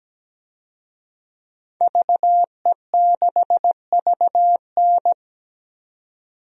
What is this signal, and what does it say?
Morse code, heard as a keyed tone, about 17 wpm, 700 Hz. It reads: VE6VN
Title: DXCC-100 Call Signs - 5 Characters - 17wpm